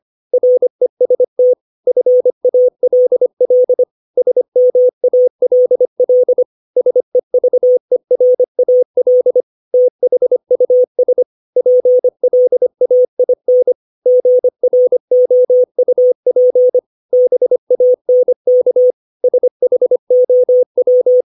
Morse code, heard as a keyed tone, about 25 wpm, 500 Hz. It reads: REST FALL SMALL SEVERAL THUS PLAIN GROUP BANK SHOW
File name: 632 words / 632 Words 25WPM